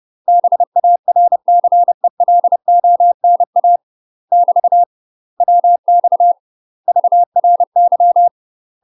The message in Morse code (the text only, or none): BARCELONA <BT> WX VRY